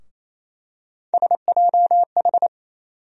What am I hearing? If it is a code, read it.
Morse code, heard as a keyed tone, about 28 wpm, 700 Hz. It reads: SJH
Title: USA Call Sign Postfixes - 3 Letters 28wpm